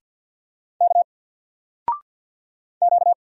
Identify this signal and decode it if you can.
Morse code, heard as a keyed tone, about 50 wpm, 700 Hz. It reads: K Q
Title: Single Letter-Number 50wpm